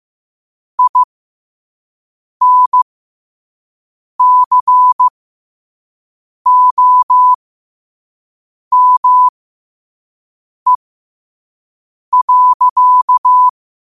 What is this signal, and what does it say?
Morse code, heard as a keyed tone, about 15 wpm, spaced as for 5 wpm, 1000 Hz. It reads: INCOME.
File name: qotdfiles / 5wpm